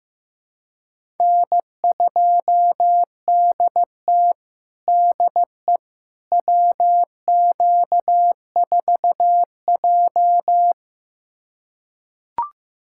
Morse code, heard as a keyed tone, about 15 wpm, 700 Hz. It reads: N2DT DE WQ4J
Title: USA Callsign DE Callsign - 4 Characters - 15wpm